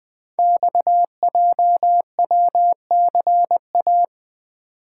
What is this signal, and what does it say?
Morse code, heard as a keyed tone, about 20 wpm, 700 Hz. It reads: XJWCA
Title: Code Groups - 5 Letter:Numbers - 20wpm